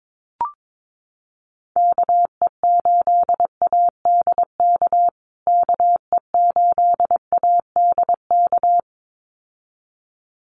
Morse code, heard as a keyed tone, about 22 wpm, 700 Hz. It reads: KE8ADK KE8ADK
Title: USA Call Signs Repeated 2x 22wpm